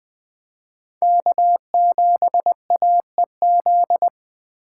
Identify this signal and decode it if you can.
Morse code, heard as a keyed tone, about 23 wpm, 700 Hz. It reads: K7AEZ
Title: Speed Racing - USA Call Signs 20wpm